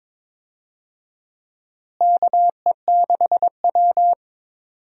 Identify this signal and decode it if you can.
Morse code, heard as a keyed tone, about 22 wpm, 700 Hz. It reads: KE6W
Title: USA Call Signs - 4 Characters - Repeated 2x - 22wpm